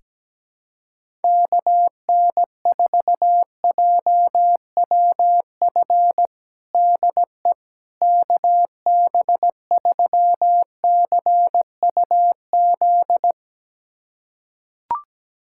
Morse code, heard as a keyed tone, about 17 wpm, 700 Hz. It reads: KN4JWF DE KB3CUZ
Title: USA Callsign DE Callsign 17wpm